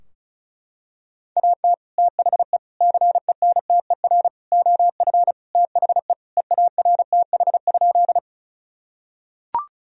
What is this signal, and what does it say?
Morse code, heard as a keyed tone, about 36 wpm, 700 Hz. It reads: AT THE CENTER OF THE EARTH?
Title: Sentences from Top 1000 Words - Review Part 2 - 35wpm